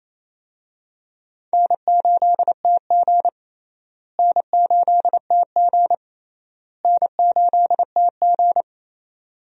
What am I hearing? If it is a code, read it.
Morse code, heard as a keyed tone, about 28 wpm, 700 Hz. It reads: N8TG N8TG N8TG
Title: USA Call Signs - 4 Characters - Repeated 3x - 28wpm